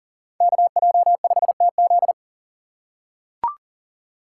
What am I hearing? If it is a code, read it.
Morse code, heard as a keyed tone, about 40 wpm, 700 Hz. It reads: KJ5TZ
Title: USA Call Signs - 5 Characters - 3x Character Spacing - 40wpm